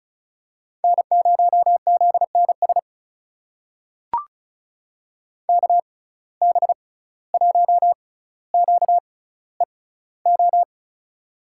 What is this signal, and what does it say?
Morse code, heard as a keyed tone, about 36 wpm, spaced as for 17 wpm, 700 Hz. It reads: N0ZNS KB1QEO
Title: USA Call Signs - 6x Character Spacing - 35wpm